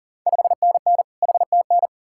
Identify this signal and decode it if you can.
Morse code, heard as a keyed tone, about 39 wpm, 690 Hz. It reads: 5NN HTN
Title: Ontario QSO Party Exchange 40wpm